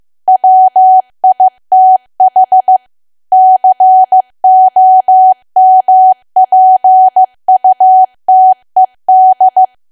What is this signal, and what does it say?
Morse code, heard as a keyed tone, about 15 wpm, 750 Hz. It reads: WITH COMPUTED
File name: cwa / W1AW-Level2 / 15WPM